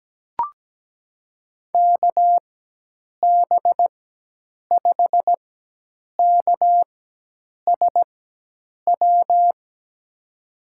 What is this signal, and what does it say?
Morse code, heard as a keyed tone, about 17 wpm, spaced as for 8 wpm, 700 Hz. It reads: KB5KSW
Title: USA Call Signs - 4x Character Spacing - 17wpm